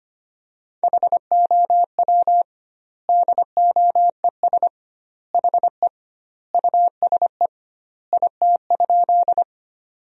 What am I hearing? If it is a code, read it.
Morse code, heard as a keyed tone, about 25 wpm, 700 Hz. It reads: HOW DOES HE USE IT?
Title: 2x Word-Spacing - Sentences from Top 200 Words - 25wpm